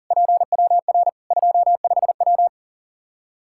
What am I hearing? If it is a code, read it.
Morse code, heard as a keyed tone, about 40 wpm, 700 Hz. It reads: PWR 25W
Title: K1USN SST QSO Practice 40wpm